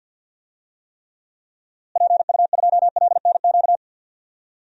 Morse code, heard as a keyed tone, about 48 wpm, 700 Hz. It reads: PU2LNQ
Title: DXCC-100 Call Signs - 6 Characters - Repeated 2x - 50wpm